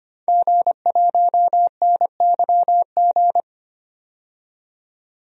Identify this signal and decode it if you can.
Morse code, heard as a keyed tone, about 25 wpm, 700 Hz. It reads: G1NYG
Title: DXCC-100 Call Signs - 5 Characters - 25wpm